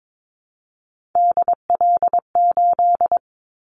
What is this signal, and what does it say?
Morse code, heard as a keyed tone, about 22 wpm, 700 Hz. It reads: DL8
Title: Callsign Prefixes for Top 25 DXCC - Repeated 3x - 22wpm